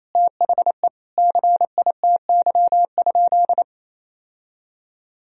Morse code, heard as a keyed tone, about 28 wpm, 700 Hz. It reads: THE CITY?
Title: Sentences from Top 1000 Words - Review Part 2 - 28wpm